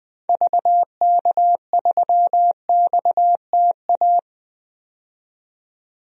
Morse code, heard as a keyed tone, about 21 wpm, 700 Hz. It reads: VK3XTA
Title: DXCC-100 Call Signs - 6 Characters - 20wpm